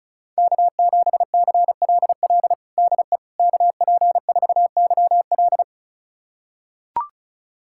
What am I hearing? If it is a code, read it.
Morse code, heard as a keyed tone, about 36 wpm, 700 Hz. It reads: K7CLL DE KP4YL
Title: USA Callsign DE Callsign - 5 Characters - 35wpm